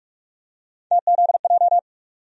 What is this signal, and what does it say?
Morse code, heard as a keyed tone, about 46 wpm, 700 Hz. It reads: TZJ
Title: USA Call Sign Postfixes - 3 Letters 45wpm